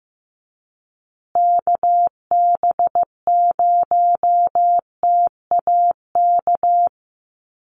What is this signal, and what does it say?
Morse code, heard as a keyed tone, about 15 wpm, 700 Hz. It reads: KB0TAK
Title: USA Call Signs Repeated 3x 15wpm